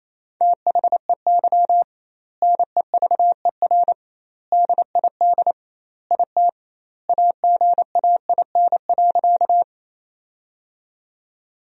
Morse code, heard as a keyed tone, about 28 wpm, 700 Hz. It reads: THEY NEVER DID IT AGAIN.
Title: 2x Word-Spacing - Sentences from Top 300 Words - 28wpm